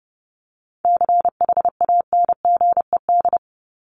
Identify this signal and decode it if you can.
Morse code, heard as a keyed tone, about 30 wpm, 700 Hz. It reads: CHANGED
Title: ED Suffix - Top 100 Words with Base Words 30wpm